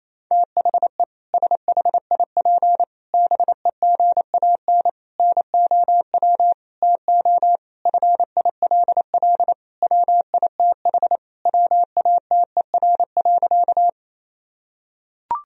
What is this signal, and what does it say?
Morse code, heard as a keyed tone, about 28 wpm, 700 Hz. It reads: THE SHIP BEGAN NOW TO FILL WITH WATER.
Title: Sentences from Top 1000 Words - Review Part 3 - 28wpm